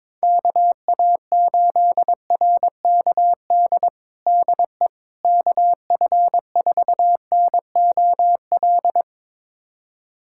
Morse code, heard as a keyed tone, about 22 wpm, 700 Hz. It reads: KA8RKD DE KF4NOL
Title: USA Callsign DE Callsign - 6 Characters - 22wpm